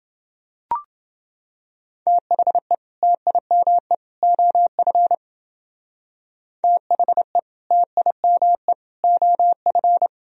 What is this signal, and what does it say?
Morse code, heard as a keyed tone, about 30 wpm, 700 Hz. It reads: THE TIME OF THE TIME OF